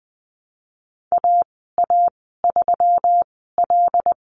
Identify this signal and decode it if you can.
Morse code, heard as a keyed tone, about 20 wpm, spaced as for 15 wpm, 700 Hz. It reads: AA3L
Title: 140 Lesson 23g - Letter f - Call Signs 20wpm